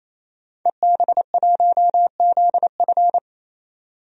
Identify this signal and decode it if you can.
Morse code, heard as a keyed tone, about 28 wpm, 700 Hz. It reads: EB1ZF
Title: DXCC-100 Call Signs - 5 Characters - 28wpm